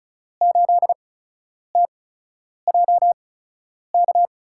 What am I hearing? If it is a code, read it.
Morse code, heard as a keyed tone, about 36 wpm, spaced as for 10 wpm, 700 Hz. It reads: 8TJK
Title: USA Call Signs - 6 Characters - 8x Character Spacing - 35wpm